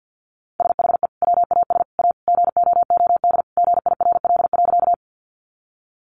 Morse code, heard as a keyed tone, about 50 wpm, 700 Hz. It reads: SHE WAS A GOOD GIRL.
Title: Sentences from Top 400 Words 50wpm